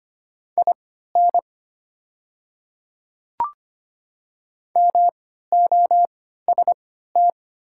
Morse code, heard as a keyed tone, about 25 wpm, spaced as for 14 wpm, 700 Hz. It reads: IN MOST
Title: 3x Character Spacing - Top 100 Words - 25wpm